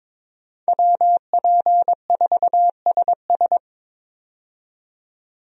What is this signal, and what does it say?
Morse code, heard as a keyed tone, about 22 wpm, 700 Hz. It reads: WP4SS
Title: USA Call Signs - 5 Characters - 22wpm